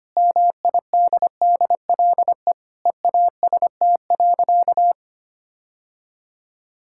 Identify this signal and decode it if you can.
Morse code, heard as a keyed tone, about 25 wpm, 700 Hz. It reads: MIDDLE EAST.
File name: practice / 25wpm / Sentences with 3-Word Phrases - 25wpm